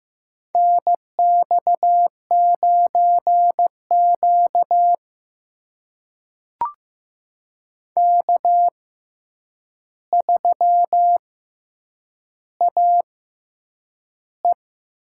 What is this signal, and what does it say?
Morse code, heard as a keyed tone, about 15 wpm, spaced as for 8 wpm, 700 Hz. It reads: NX9Q K3AE
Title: USA Call Signs - 4 Characters - 6x Character Spacing - 15wpm